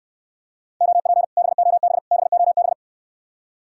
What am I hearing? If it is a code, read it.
Morse code, heard as a keyed tone, about 69 wpm, 700 Hz. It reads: OP BOB BOB